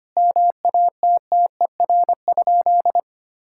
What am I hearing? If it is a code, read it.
Morse code, heard as a keyed tone, about 25 wpm, 700 Hz. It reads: MATTER?
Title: Sentences from Top 1000 Words - Review Part 6 - 25wpm